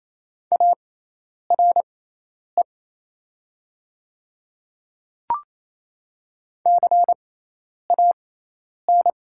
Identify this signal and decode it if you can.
Morse code, heard as a keyed tone, about 28 wpm, spaced as for 10 wpm, 690 Hz. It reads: ARE CAN
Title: 6x Character Spacing - Top 200 Words - 28wpm